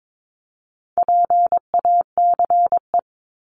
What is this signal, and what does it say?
Morse code, heard as a keyed tone, about 22 wpm, 700 Hz. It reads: PACE
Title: Top 1751-2000 Words 22wpm